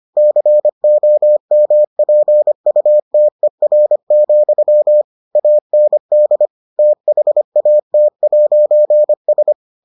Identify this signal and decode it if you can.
Morse code, heard as a keyed tone, about 25 wpm, 600 Hz. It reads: COMPUTER, AND THAT'S